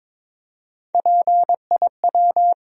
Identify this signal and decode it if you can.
Morse code, heard as a keyed tone, about 22 wpm, 700 Hz. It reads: PIW
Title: USA Call Sign Postfixes - 3 Letters 22wpm